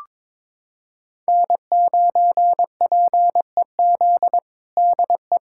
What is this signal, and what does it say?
Morse code, heard as a keyed tone, about 22 wpm, 700 Hz. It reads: N9PEZ DE
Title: USA Callsign DE Callsign - 5 Characters - 22wpm